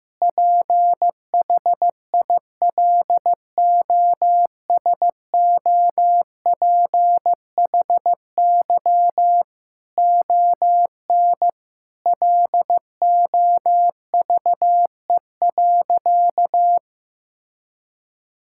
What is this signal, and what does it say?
Morse code, heard as a keyed tone, about 15 wpm, 700 Hz. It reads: PHILOSOPHY ON LOVE.
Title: Sentences from Top 1000 Words - Review Part 6 - 15wpm